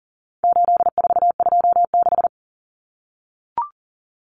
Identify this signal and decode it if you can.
Morse code, heard as a keyed tone, about 40 wpm, 700 Hz. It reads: 8426
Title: Numbers - 4-Digit 40wpm